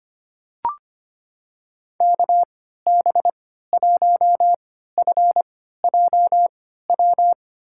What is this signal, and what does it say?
Morse code, heard as a keyed tone, about 25 wpm, spaced as for 14 wpm, 700 Hz. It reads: KB1FJW
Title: USA Call Signs - 3x Character Spacing - 25wpm